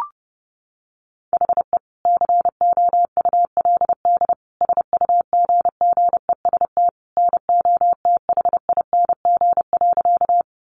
Morse code, heard as a keyed tone, about 28 wpm, 700 Hz. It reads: HE COULD SUGGEST NOTHING.
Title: Sentences from Top 1300 Words 30wpm